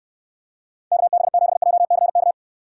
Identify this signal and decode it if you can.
Morse code, heard as a keyed tone, about 69 wpm, 700 Hz. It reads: KB7JCK